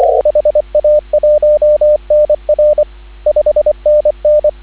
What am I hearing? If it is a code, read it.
Morse code, heard as a keyed tone, about 24 wpm, 590 Hz, with radio noise. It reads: HA1NR 5NN
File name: CW / 2017 HA-OB HA5YG